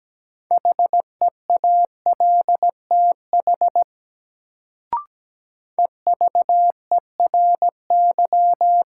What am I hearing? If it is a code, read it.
Morse code, heard as a keyed tone, about 17 wpm, 700 Hz. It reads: HEALTH EVERY
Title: Sets of 5 Words - New Word Drill - 17wpm